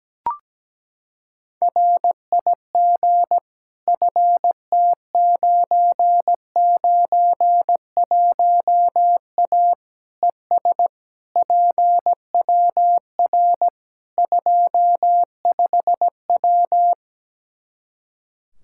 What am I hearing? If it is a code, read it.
Morse code, heard as a keyed tone, about 17 wpm, 700 Hz. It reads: RIG FT991A ES PWR 25W
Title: Practice QSO - Exchange 2 - Info - 17wpm